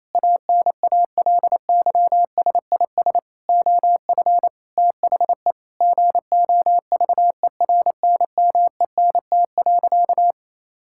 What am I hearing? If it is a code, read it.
Morse code, heard as a keyed tone, about 28 wpm, 700 Hz. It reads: ANALYSIS OF THE GOVERNMENT.
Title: Sentences from Top 1000 Words - Review Part 5 - 28wpm